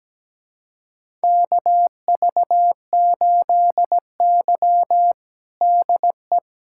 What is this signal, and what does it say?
Morse code, heard as a keyed tone, about 17 wpm, 700 Hz. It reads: KV8Y DE
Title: USA Callsign DE Callsign - 4 Characters - 17wpm